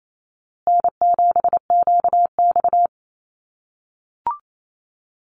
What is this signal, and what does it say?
Morse code, heard as a keyed tone, about 28 wpm, 700 Hz. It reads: N7QX
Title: USA Call Signs Repeated 3x 28wpm